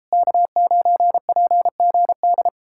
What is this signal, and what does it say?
Morse code, heard as a keyed tone, about 38 wpm, 700 Hz. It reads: K9PGD